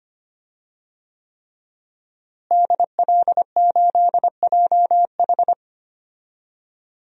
Morse code, heard as a keyed tone, about 25 wpm, 700 Hz. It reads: DL8JH
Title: DXCC-100 Call Signs - 5 Characters - Repeated 3x - 25wpm